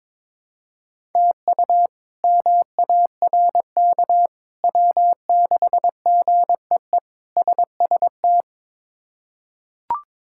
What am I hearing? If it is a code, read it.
Morse code, heard as a keyed tone, about 22 wpm, 700 Hz. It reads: TU MARK W6GEE SST
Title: K1USN SST QSO Practice - 22wpm